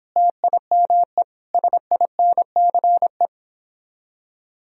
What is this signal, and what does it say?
Morse code, heard as a keyed tone, about 26 wpm, 700 Hz. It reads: TIME SINCE